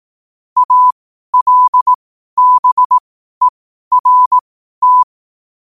Morse code, heard as a keyed tone, about 18 wpm, spaced as for 12 wpm, 1000 Hz. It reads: ALBERT
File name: qotdfiles / 12wpm